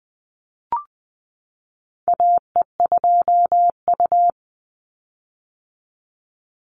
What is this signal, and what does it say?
Morse code, heard as a keyed tone, about 20 wpm, 700 Hz. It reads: AE2U
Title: Call Signs - 20wpm